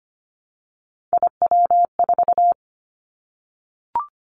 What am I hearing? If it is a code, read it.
Morse code, heard as a keyed tone, about 25 wpm, 700 Hz. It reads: IW4
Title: Callsign Prefixes for Top 50 DXCC - 25wpm